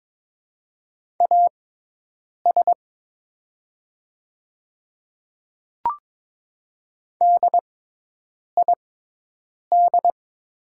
Morse code, heard as a keyed tone, about 22 wpm, spaced as for 8 wpm, 700 Hz. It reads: AS DID
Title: 6x Character Spacing - Top 100 Words - 22wpm